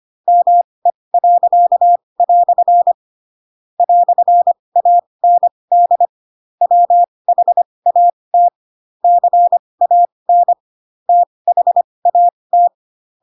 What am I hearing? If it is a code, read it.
Morse code, heard as a keyed tone, about 25 wpm, spaced as for 20 wpm, 700 Hz. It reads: ME." "AND WHAT CAN THAT
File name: Pinocchio / Pinocchio ch4 15 25 ik1hge